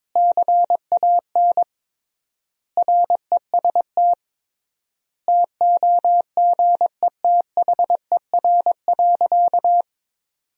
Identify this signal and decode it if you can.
Morse code, heard as a keyed tone, about 22 wpm, 700 Hz. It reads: CAN REST TOGETHER.
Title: 3x Word-Spacing - Sentences from Top 500 Words - 22wpm